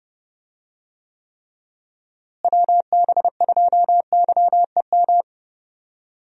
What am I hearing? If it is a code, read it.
Morse code, heard as a keyed tone, about 30 wpm, 700 Hz. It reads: WB2YEM
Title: USA Call Signs - 6 Characters - Repeated 3x - 30wpm